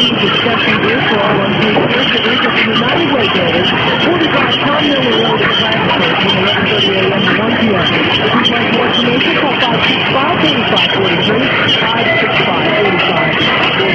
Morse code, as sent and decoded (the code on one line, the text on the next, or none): none